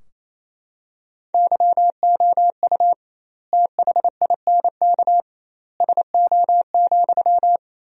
Morse code, heard as a keyed tone, about 27 wpm, 700 Hz. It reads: YOU THINK SO,
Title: 2x Word-Spacing - Sentences from Top 200 Words - 28wpm